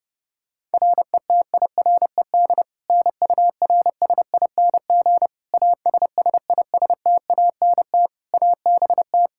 Morse code, heard as a keyed tone, about 32 wpm, 700 Hz. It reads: RETIRED NURSING ASSISTANT ABT
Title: Practice QSO - Exchange 3 - More Info - 30wpm